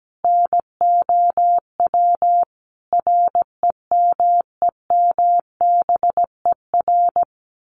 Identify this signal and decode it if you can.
Morse code, heard as a keyed tone, about 17 wpm, 700 Hz. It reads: NOW REMEMBER
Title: Sentences from Top 1200 Words 17wpm